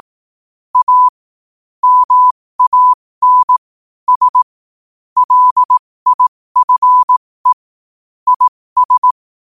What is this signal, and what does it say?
Morse code, heard as a keyed tone, about 18 wpm, spaced as for 15 wpm, 1000 Hz. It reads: A MAN S LIFE IS
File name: qotdfiles / 15wpm